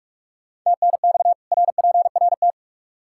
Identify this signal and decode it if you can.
Morse code, heard as a keyed tone, about 47 wpm, 700 Hz. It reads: TNX RPRT
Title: Practice QSO - Exchange 2 - Info - 45wpm